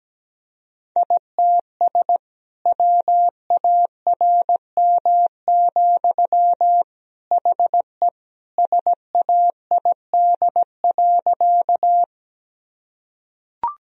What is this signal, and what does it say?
Morse code, heard as a keyed tone, about 17 wpm, 700 Hz. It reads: ITS WARM, HE SAID.
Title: Sentences from Top 1000 Words - Review Part 2 - 17wpm